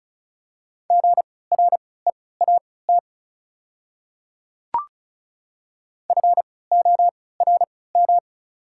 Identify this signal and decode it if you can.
Morse code, heard as a keyed tone, about 34 wpm, spaced as for 20 wpm, 690 Hz. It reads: GREAT FORM
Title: 3x Character Spacing - Top 300 Words - 35wpm